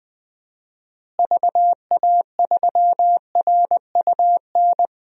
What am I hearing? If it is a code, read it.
Morse code, heard as a keyed tone, about 20 wpm, 700 Hz. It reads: VA3RUN
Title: DXCC-100 Call Signs - 6 Characters - Repeated 3x - 20wpm